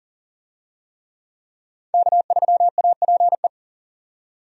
Morse code, heard as a keyed tone, about 40 wpm, 700 Hz. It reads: K3APE
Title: USA Call Signs - 5 Characters - 40wpm